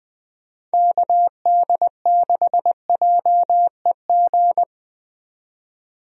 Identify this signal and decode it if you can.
Morse code, heard as a keyed tone, about 20 wpm, 700 Hz. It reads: KD6JEG
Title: USA Call Signs - 6 Characters - 20wpm